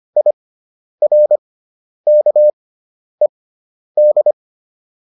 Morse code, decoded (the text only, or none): IRKED